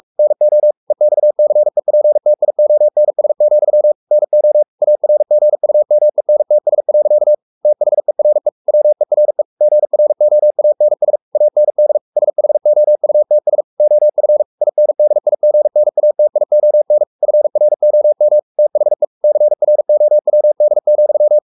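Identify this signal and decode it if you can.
Morse code, heard as a keyed tone, about 44 wpm, 600 Hz. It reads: NO EXCEPTIONS, NO ARGUMENTS. THERE WERE GROANS AND SHOUTS OF INDIGNATION FROM THE CROWD,
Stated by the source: Morse code at 43 wpm sending Ken Follett's novel Never in English, from CW med Gnister.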